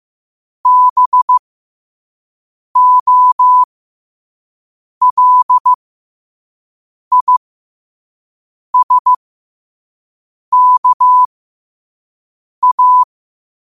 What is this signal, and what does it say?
Morse code, heard as a keyed tone, about 15 wpm, spaced as for 5 wpm, 1000 Hz. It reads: BOLISKA
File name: qotdfiles / 5wpm